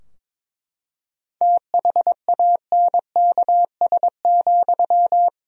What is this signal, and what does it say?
Morse code, heard as a keyed tone, about 22 wpm, 700 Hz. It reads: THANKS,
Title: Sentences from Top 1000 Words - Review Part 3 - 22wpm